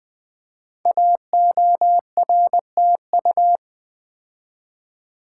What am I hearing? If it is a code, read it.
Morse code, heard as a keyed tone, about 20 wpm, 700 Hz. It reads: AORTU
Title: Code Groups - 5 Letters - 20wpm